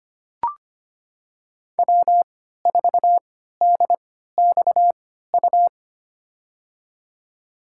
Morse code, heard as a keyed tone, about 25 wpm, spaced as for 14 wpm, 700 Hz. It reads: W4DXU
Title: USA Call Signs - 5 Characters - 3x Character Spacing - 25wpm